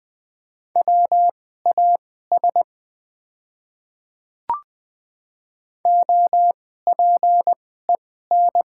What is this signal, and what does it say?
Morse code, heard as a keyed tone, about 21 wpm, spaced as for 15 wpm, 690 Hz. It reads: WAS OPEN
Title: 2x Character Spacing - Top 300 Words - 20wpm